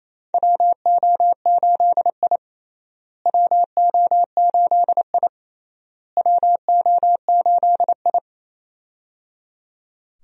Morse code, heard as a keyed tone, about 28 wpm, 700 Hz. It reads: WO8I WO8I WO8I
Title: USA Call Signs - 4 Characters - Repeated 3x - 28wpm